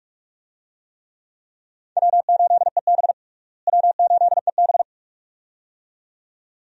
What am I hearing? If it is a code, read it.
Morse code, heard as a keyed tone, about 46 wpm, 700 Hz. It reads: W8EB W8EB
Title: USA Call Signs - 4 Characters - Repeated 2x - 45wpm